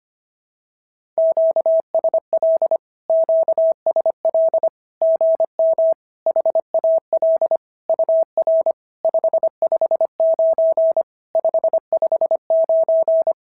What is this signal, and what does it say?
Morse code, heard as a keyed tone, about 25 wpm, 650 Hz. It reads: QSL QSL GM HAL UR 559 559
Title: SOTA QSO Practice 25wpm